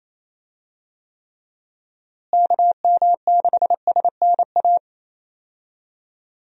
Morse code, heard as a keyed tone, about 27 wpm, 700 Hz. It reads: KM6SNA
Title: USA Call Signs - 6 Characters - Repeated 3x - 28wpm